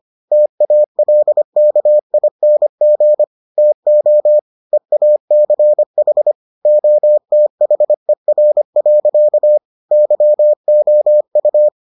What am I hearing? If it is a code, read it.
Morse code, heard as a keyed tone, about 25 wpm, 600 Hz. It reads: TALKING TO EACH OTHER. YOU